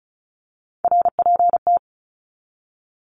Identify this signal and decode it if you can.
Morse code, heard as a keyed tone, about 34 wpm, 700 Hz. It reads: RPT
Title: Abbreviations 35wpm